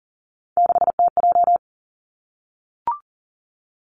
Morse code, heard as a keyed tone, about 42 wpm, 700 Hz. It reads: 6TJ
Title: Three Challenging Letter-Number Combinations 40wpm